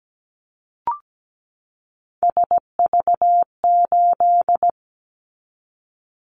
Morse code, heard as a keyed tone, about 17 wpm, 700 Hz. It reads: SV8
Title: Callsign Prefixes for Top 50 DXCC - 17wpm